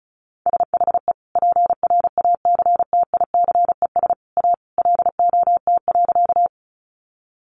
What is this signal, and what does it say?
Morse code, heard as a keyed tone, about 34 wpm, 700 Hz. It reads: SHE PRACTICES A LOT.
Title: Additional Easy Phrases 35wpm